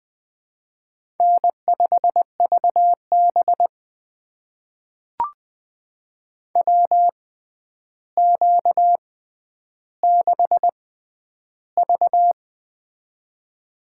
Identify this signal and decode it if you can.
Morse code, heard as a keyed tone, about 20 wpm, spaced as for 10 wpm, 700 Hz. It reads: N5VB WQ6V
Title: USA Call Signs - 4 Characters - 6x Character Spacing - 20wpm